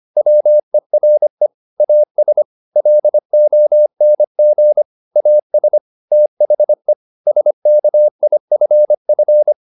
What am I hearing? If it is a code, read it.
Morse code, heard as a keyed tone, about 25 wpm, 600 Hz. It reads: WERE AS LONG AS THE SKIFF
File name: the old man and the sea part 1 25wpm